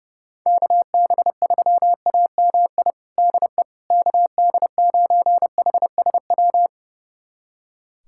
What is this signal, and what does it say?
Morse code, heard as a keyed tone, about 31 wpm, 700 Hz. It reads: KB3AMI DE KD9HSW
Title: USA Callsign DE Callsign - 6 Characters - 30wpm